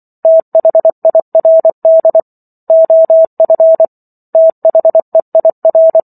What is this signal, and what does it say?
Morse code, heard as a keyed tone, about 24 wpm, 650 Hz. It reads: THIRD OF THEIR